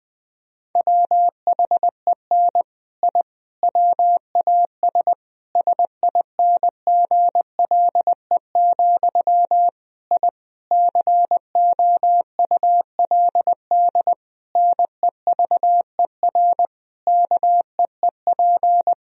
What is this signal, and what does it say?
Morse code, heard as a keyed tone, about 20 wpm, 700 Hz. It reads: WHEN I WAS SINGLE, I COULD NEVER KEEP
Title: Sentences from Top 1000 Words - Review Part 4 - 20wpm